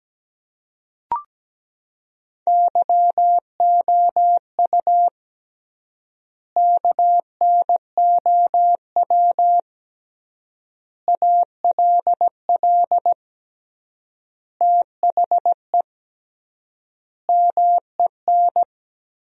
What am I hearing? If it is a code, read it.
Morse code, heard as a keyed tone, about 17 wpm, 700 Hz. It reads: YOU KNOW ALL THE MEN
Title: 3x Word-Spacing - Sentences from Top 300 Words - 17wpm